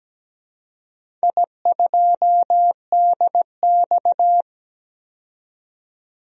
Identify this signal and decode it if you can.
Morse code, heard as a keyed tone, about 17 wpm, 700 Hz. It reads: I2DX
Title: DXCC-100 Call Signs - 4 Characters - 17wpm